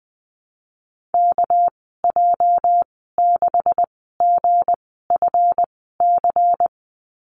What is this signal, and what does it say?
Morse code, heard as a keyed tone, about 20 wpm, spaced as for 14 wpm, 700 Hz. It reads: KJ6GFC
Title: 219 Lesson 35g - Number 0 - Call Signs 20wpm